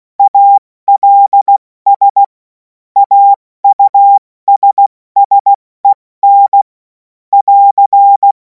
Morse code, heard as a keyed tone, about 16 wpm, 800 Hz. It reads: ALS AUSSEN <AR>
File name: Morsekurs / Witze / Witz 14. April 2026 70 F80